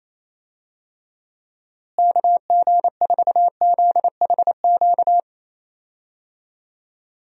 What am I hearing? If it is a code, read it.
Morse code, heard as a keyed tone, about 28 wpm, 700 Hz. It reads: KG4ZHQ
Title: USA Call Signs - 6 Characters - Repeated 3x - 28wpm